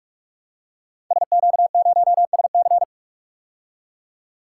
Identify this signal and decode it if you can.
Morse code, heard as a keyed tone, about 46 wpm, 700 Hz. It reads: SQ0SC
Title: Code Groups - 5 Letter:Numbers - 45wpm